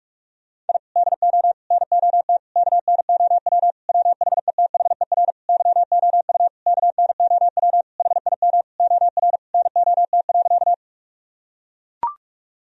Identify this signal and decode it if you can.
Morse code, heard as a keyed tone, about 41 wpm, 700 Hz. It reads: I DO NOT KNOW WHETHER YOU KNOW HIM OR NOT.
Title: Sentences from Top 1000 Words - Review Part 3 - 45wpm